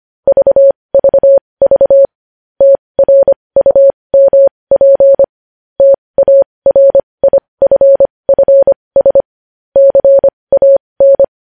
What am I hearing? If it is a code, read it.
Morse code, heard as a keyed tone, about 25 wpm, spaced as for 20 wpm, 550 Hz. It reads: VVV TRUMP TARIFFS CAN
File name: podcast / News headlines-1x-25x20